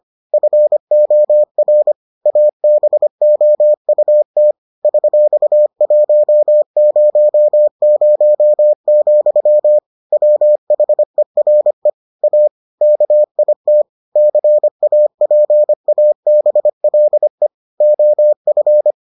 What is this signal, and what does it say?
Morse code, heard as a keyed tone, about 25 wpm, 600 Hz. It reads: FOR ABOUT $100, WHERE A KIT CAPABLE OF